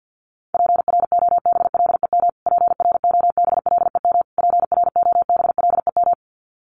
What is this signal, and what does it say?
Morse code, heard as a keyed tone, about 50 wpm, 700 Hz. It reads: PROBLEM PROBLEM PROBLEM
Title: Top 500 Words Repeated 50wpm